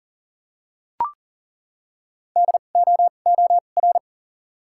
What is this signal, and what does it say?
Morse code, heard as a keyed tone, about 41 wpm, spaced as for 29 wpm, 700 Hz. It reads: DOOR
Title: 2x Character Spacing - Top 300 Words - 40wpm